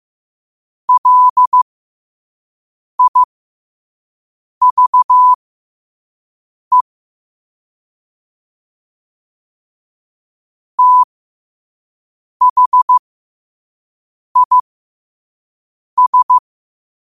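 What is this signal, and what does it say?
Morse code, heard as a keyed tone, about 15 wpm, spaced as for 5 wpm, 1000 Hz. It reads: LIVE THIS
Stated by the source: Morse code practice at 5 wpm.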